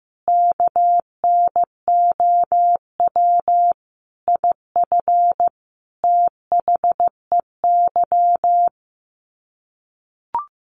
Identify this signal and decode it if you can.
Morse code, heard as a keyed tone, about 15 wpm, 700 Hz. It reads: KNOW IF THEY
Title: Sets of 5 Words Encore 15wpm